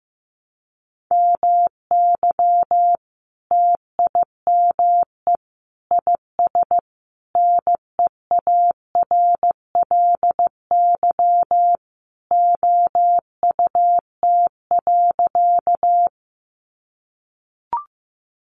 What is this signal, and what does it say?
Morse code, heard as a keyed tone, about 15 wpm, 700 Hz. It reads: MY TIME IS NEARLY OUT.
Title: Sentences from Top 1300 Words 15wpm